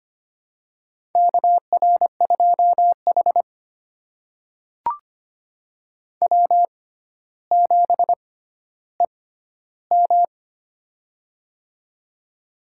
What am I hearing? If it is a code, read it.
Morse code, heard as a keyed tone, about 25 wpm, spaced as for 13 wpm, 700 Hz. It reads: KR2H W7EM
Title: USA Call Signs - 4 Characters - 6x Character Spacing - 25wpm